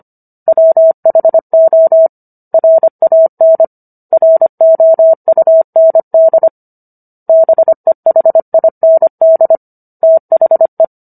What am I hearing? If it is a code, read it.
Morse code, heard as a keyed tone, about 25 wpm, 650 Hz. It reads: WHO RAN ROUND BEHIND THE